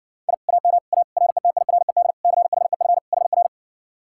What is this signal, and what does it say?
Morse code, heard as a keyed tone, about 60 wpm, 700 Hz. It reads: I AM A RETIRED CHEF HR